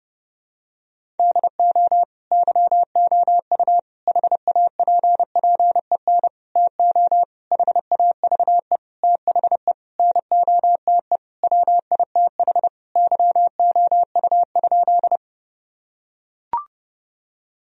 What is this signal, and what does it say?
Morse code, heard as a keyed tone, about 30 wpm, 700 Hz. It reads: DO YOU HAPPEN TO HAVE THE NOTE WITH YOU?
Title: Sentences from Top 1000 Words - Review Part 2 - 30wpm